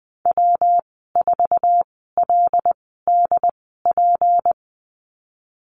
Morse code, heard as a keyed tone, about 20 wpm, spaced as for 14 wpm, 700 Hz. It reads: W4LDP
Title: 154 Lesson 25g - Letter p - Call Signs 20wpm